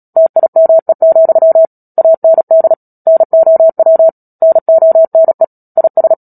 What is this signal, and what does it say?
Morse code, heard as a keyed tone, about 37 wpm, 650 Hz. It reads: TIME, AND NOW NONE IS
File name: Book Two / 10-Chapter 10 - Epilogue - 36 WPM